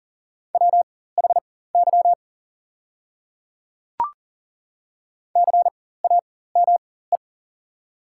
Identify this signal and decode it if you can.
Morse code, heard as a keyed tone, about 40 wpm, spaced as for 19 wpm, 690 Hz. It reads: WHY CAME
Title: 4x Character Spacing - Top 300 Words - 40wpm